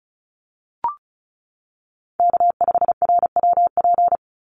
Morse code, heard as a keyed tone, about 34 wpm, 700 Hz. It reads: K5RWP
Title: NA SOTA Chaser Callsigns 35wpm